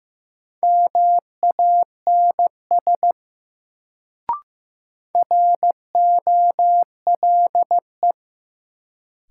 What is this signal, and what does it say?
Morse code, heard as a keyed tone, about 15 wpm, 700 Hz. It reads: MANS ROLE
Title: Sentences from Top 1000 Words - Review Part 6 - New Word Drill - 15wpm